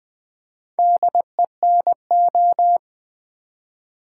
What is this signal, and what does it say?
Morse code, heard as a keyed tone, about 20 wpm, 700 Hz. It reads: DENO
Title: Build A Word - 2 Letters At A Time - 8-Letter Words - 20wpm